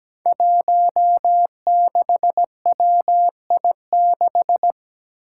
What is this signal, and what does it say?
Morse code, heard as a keyed tone, about 17 wpm, 700 Hz. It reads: 16WI6
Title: Five Challenging Letter-Number Combinations 17wpm